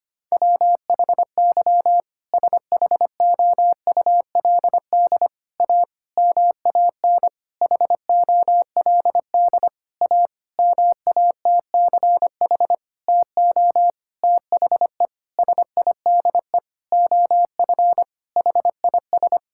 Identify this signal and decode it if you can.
Morse code, heard as a keyed tone, about 25 wpm, 700 Hz. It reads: WHY SHOULD A MAN HOLD A MATCH TO THE SIDE OF HIS
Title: Sentences from Top 1600 Words 25wpm